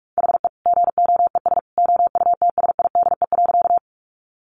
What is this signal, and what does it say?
Morse code, heard as a keyed tone, about 44 wpm, 700 Hz. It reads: HE GOES OUTSIDE.
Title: Additional Easy Phrases 45wpm